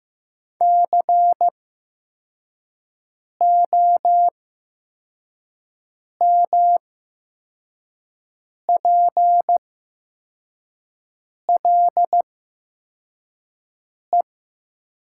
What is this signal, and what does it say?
Morse code, heard as a keyed tone, about 15 wpm, spaced as for 4 wpm, 700 Hz. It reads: COMPLE
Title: 8x Character Spacing - Top 500 Words - 15wpm